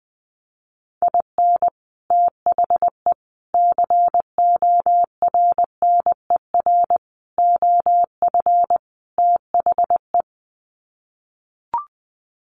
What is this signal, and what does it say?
Morse code, heard as a keyed tone, about 20 wpm, 700 Hz. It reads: IN THE CORNER OF THE
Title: Sets of 5 Words Encore 20wpm